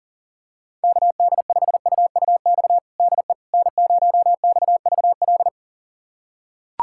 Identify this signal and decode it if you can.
Morse code, heard as a keyed tone, about 40 wpm, 700 Hz. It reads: KD5UUX DE N0XVL
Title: USA Callsign DE Callsign 40wpm